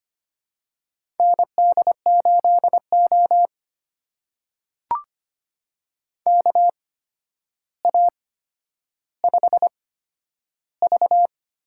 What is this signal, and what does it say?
Morse code, heard as a keyed tone, about 25 wpm, spaced as for 11 wpm, 700 Hz. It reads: ND8O KA5V
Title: USA Call Signs - 4 Characters - 8x Character Spacing - 25wpm